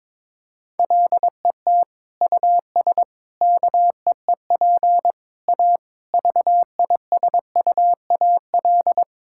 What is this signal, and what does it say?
Morse code, heard as a keyed tone, about 22 wpm, 700 Hz. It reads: LET US KEEP A VISUAL
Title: Sentences from Top 1000 Words - Review Part 6 - 22wpm